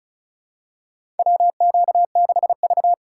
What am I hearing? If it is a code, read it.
Morse code, heard as a keyed tone, about 36 wpm, 700 Hz. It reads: WQ6V
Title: USA Call Signs - 4 Characters - 6x Character Spacing - 35wpm